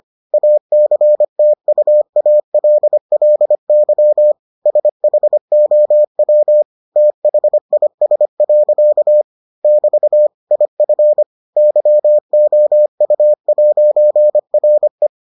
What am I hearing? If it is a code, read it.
Morse code, heard as a keyed tone, about 25 wpm, 600 Hz. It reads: ACTUALLY SHOW THIS. <BT> IF YOU'RE